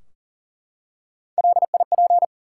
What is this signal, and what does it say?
Morse code, heard as a keyed tone, about 40 wpm, 700 Hz. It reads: LIP
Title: Three Letter Words - 40wpm